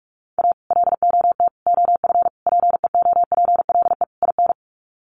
Morse code, heard as a keyed tone, about 44 wpm, 700 Hz. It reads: A LOT OF PEOPLE IN
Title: Sets of 5 Words 45wpm